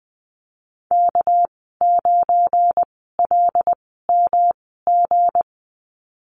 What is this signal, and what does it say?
Morse code, heard as a keyed tone, about 20 wpm, spaced as for 15 wpm, 700 Hz. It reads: K9LMG
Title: 207 Lesson 33g - Letter j - Call Signs 20wpm